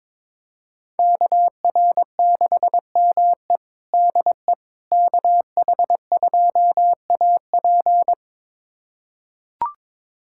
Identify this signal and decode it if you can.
Morse code, heard as a keyed tone, about 22 wpm, 700 Hz. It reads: KR6ME DE KH2AP
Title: USA Callsign DE Callsign - 5 Characters - 22wpm